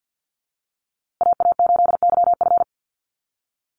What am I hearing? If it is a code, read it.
Morse code, heard as a keyed tone, about 50 wpm, 700 Hz. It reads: AA8YF
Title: USA Call Signs Repeated 3x 50wpm